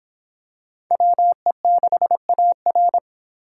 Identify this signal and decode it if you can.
Morse code, heard as a keyed tone, about 23 wpm, 700 Hz. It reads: WE6AR